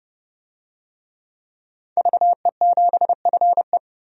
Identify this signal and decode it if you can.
Morse code, heard as a keyed tone, about 29 wpm, 700 Hz. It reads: VE7FE
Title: DXCC-100 Call Signs - 5 Characters - 30wpm